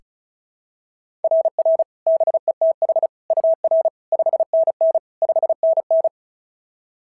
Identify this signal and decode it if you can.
Morse code, heard as a keyed tone, about 36 wpm, 690 Hz. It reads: RR BETH UR 5NN 5NN
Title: SOTA QSO Practice 35wpm